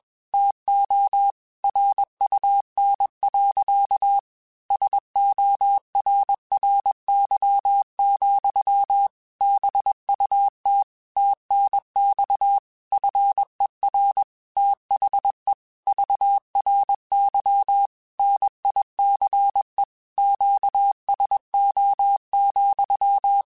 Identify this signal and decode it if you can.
Morse code, heard as a keyed tone, about 21 wpm, 800 Hz. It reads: TO RUN. SORRY, BUT TNX FER THE VRY NICE QSO,
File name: QSOs / QSOs-mp3s / kc7haa@20